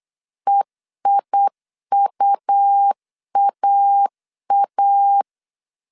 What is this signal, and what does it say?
Morse code, heard as a keyed tone, about 8 wpm, 800 Hz. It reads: EIUAA